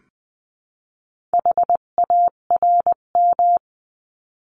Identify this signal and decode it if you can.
Morse code, heard as a keyed tone, about 20 wpm, 700 Hz. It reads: HARM
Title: 258 Lesson 42c - Words 16-20wpm